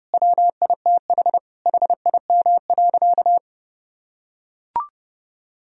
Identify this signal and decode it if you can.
Morse code, heard as a keyed tone, about 31 wpm, 700 Hz. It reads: WITH HIM.
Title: Sentences from Top 1000 Words - Review Part 4 - 30wpm